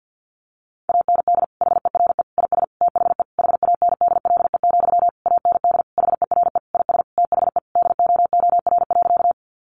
Morse code, heard as a keyed tone, about 50 wpm, 700 Hz. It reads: AND HERE IS THE HANDLE, AND HERE IS THE DOOR.
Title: Sentences from Top 1100 Words 50wpm